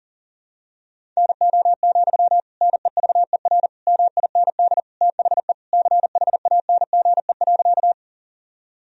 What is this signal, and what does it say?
Morse code, heard as a keyed tone, about 40 wpm, 700 Hz. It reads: NO, NEVER MIND THE CHANGE.
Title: Sentences from Top 1000 Words - Review Part 1 - 40wpm